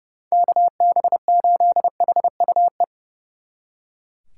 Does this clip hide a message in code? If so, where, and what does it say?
Yes: Morse code from 0.3 to 2.8 s, heard as a keyed tone, about 39 wpm, 700 Hz. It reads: KB8HUE